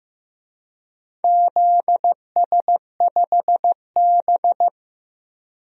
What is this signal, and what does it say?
Morse code, heard as a keyed tone, about 15 wpm, 700 Hz. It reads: ZS5B
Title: DXCC-100 Call Signs - 4 Characters - Repeated 2x - 15wpm